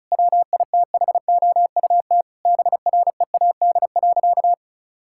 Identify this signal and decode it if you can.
Morse code, heard as a keyed tone, about 36 wpm, 700 Hz. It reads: WITHOUT BREAD.
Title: Sentences from Top 1600 Words 35wpm